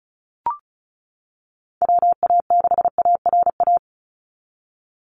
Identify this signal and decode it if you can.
Morse code, heard as a keyed tone, about 36 wpm, 700 Hz. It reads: WA6ARA
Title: NA SOTA Chaser Callsigns 35wpm